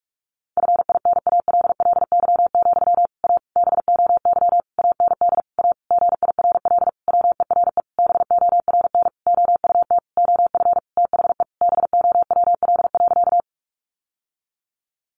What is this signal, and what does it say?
Morse code, heard as a keyed tone, about 44 wpm, 700 Hz. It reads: FINALLY, A BOY AND A GIRL WERE BORN OUT OF THE BOWL.
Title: Sentences from Top 1200 Words 45wpm